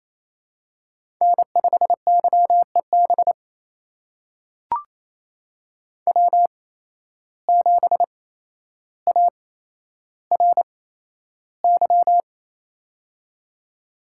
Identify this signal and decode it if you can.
Morse code, heard as a keyed tone, about 27 wpm, spaced as for 12 wpm, 700 Hz. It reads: N5YEB W7ARY
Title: USA Call Signs - 5 Characters - 8x Character Spacing - 28wpm